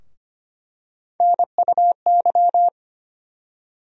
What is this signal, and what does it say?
Morse code, heard as a keyed tone, about 25 wpm, 700 Hz. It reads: NUY
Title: USA Call Sign Postfixes - 3 Letters 25wpm